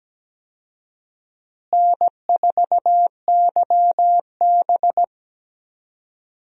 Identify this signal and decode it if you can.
Morse code, heard as a keyed tone, about 17 wpm, 700 Hz. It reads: N4YB
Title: USA Call Signs - 4 Characters - Repeated 3x - 17wpm